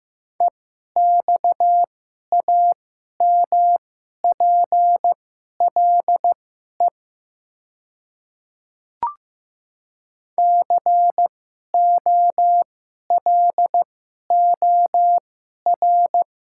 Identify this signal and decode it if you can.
Morse code, heard as a keyed tone, about 15 wpm, spaced as for 11 wpm, 700 Hz. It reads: EXAMPLE COLOR
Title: 2x Character Spacing - Top 300 Words - 15wpm